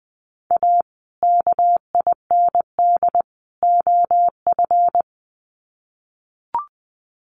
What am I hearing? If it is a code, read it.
Morse code, heard as a keyed tone, about 20 wpm, 700 Hz. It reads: A KIND OF
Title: Sets of 3 Words 20wpm